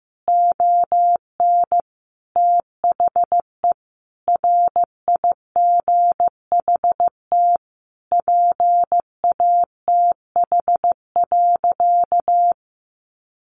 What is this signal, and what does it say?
Morse code, heard as a keyed tone, about 15 wpm, 700 Hz. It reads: ON THE RIGHT PATH.
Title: Sentences from Top 1400 Words 15wpm